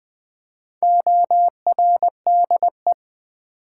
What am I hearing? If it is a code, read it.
Morse code, heard as a keyed tone, about 20 wpm, 700 Hz. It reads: ORDE
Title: Build A Word - 1 Letter At A Time - 3 to 5-Letter Words - 20wpm